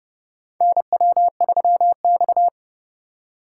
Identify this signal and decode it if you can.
Morse code, heard as a keyed tone, about 30 wpm, 700 Hz. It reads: NW3X
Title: USA Call Signs - 4 Characters - 3x Character Spacing - 30wpm